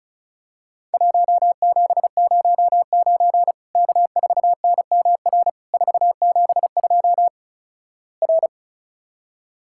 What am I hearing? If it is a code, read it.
Morse code, heard as a keyed tone, about 34 wpm, 700 Hz. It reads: 1709 K4NMR 472 R
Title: Worked All Europe DX Contest - QTC Exchange - 35wpm